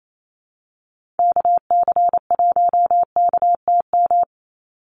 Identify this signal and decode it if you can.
Morse code, heard as a keyed tone, about 28 wpm, 700 Hz. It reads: KC1KTM
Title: USA Call Signs 28wpm